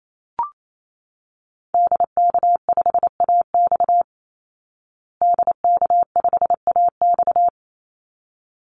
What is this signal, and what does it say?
Morse code, heard as a keyed tone, about 28 wpm, 700 Hz. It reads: DK5AX DK5AX
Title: Call Signs Repeated 3x - 28wpm